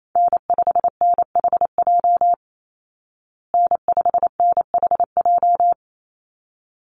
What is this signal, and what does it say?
Morse code, heard as a keyed tone, about 28 wpm, 690 Hz. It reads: N5NHJ N5NHJ
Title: Call Signs Repeated 3x - 28wpm